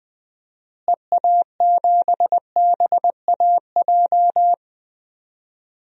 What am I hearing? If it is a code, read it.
Morse code, heard as a keyed tone, about 20 wpm, 700 Hz. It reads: EA7BAJ
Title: DXCC-100 Call Signs - 6 Characters - Repeated 3x - 20wpm